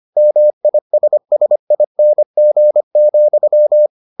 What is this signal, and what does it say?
Morse code, heard as a keyed tone, about 25 wpm, 600 Hz. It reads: MISSING,